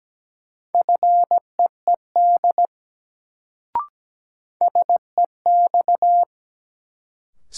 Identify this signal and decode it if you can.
Morse code, heard as a keyed tone, about 17 wpm, 700 Hz. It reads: FEED SEX
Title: Sentences from Top 1600 Words - New Word Drill - 17wpm